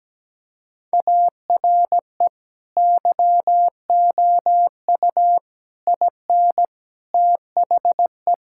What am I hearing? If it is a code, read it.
Morse code, heard as a keyed tone, about 17 wpm, 700 Hz. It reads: ARE YOU IN THE
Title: Sentences from Top 1000 Words - Review Part 6 - 17wpm